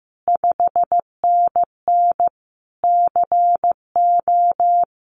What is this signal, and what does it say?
Morse code, heard as a keyed tone, about 16 wpm, 700 Hz. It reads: 5NN CO
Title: Common QSO Phrases Encore 15wpm